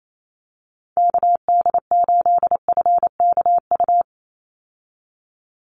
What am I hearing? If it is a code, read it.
Morse code, heard as a keyed tone, about 28 wpm, 700 Hz. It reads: KD8FKU
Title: USA Call Signs 28wpm